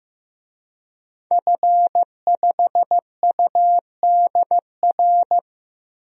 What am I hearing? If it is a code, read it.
Morse code, heard as a keyed tone, about 15 wpm, 700 Hz. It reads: F5UDR
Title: DXCC-100 Call Signs - 5 Characters - 15wpm